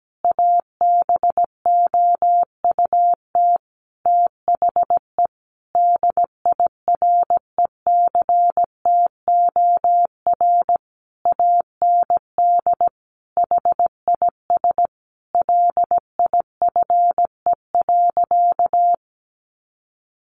Morse code, heard as a keyed tone, about 17 wpm, 700 Hz. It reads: ABOUT THE DIRECTOR AND HIS LIFE.
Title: Sentences from Top 1300 Words 17wpm